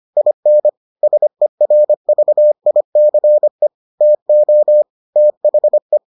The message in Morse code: IN SERVICE TO THE

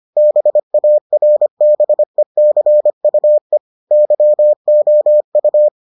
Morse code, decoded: BARBECUE YOU

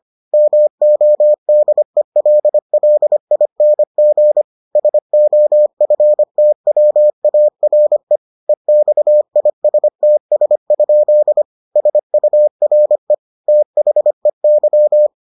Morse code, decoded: MODELLING SOFTWARE EXISTS? SURE THEY

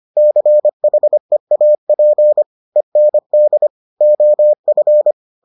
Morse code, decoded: CHEAP END OF

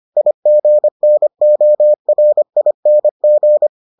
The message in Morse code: IGNORING

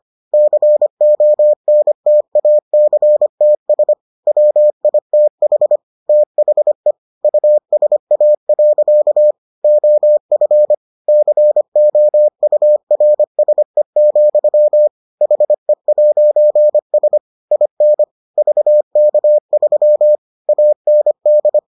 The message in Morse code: CONTACTS WITH THE USA. OF COURSE, HE'S IN VK3 AND

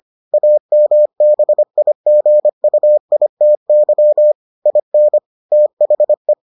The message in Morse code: AMBIGUITY IN THE

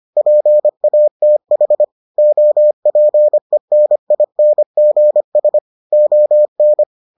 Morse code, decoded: PATH OPENINGS ON

